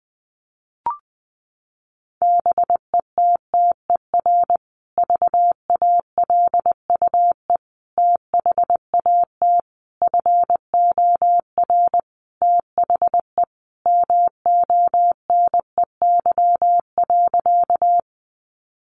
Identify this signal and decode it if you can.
Morse code, heard as a keyed tone, about 20 wpm, 700 Hz. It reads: BETTER VALUE THAT FOR THE MONEY.
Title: Sentences from Top 600 Words 20wpm